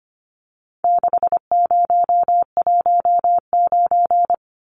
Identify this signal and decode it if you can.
Morse code, heard as a keyed tone, about 25 wpm, 700 Hz. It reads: 6019
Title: Numbers - 4-Digit 25wpm